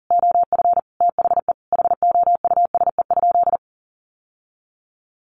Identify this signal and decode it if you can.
Morse code, heard as a keyed tone, about 40 wpm, 700 Hz. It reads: OF THE HOUSE?
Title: Sentences from Top 900 Words 40wpm